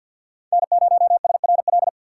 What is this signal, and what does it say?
Morse code, heard as a keyed tone, about 53 wpm, 700 Hz. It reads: N0SRL
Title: USA Call Signs - 5 Characters - 2x Character Spacing - 50wpm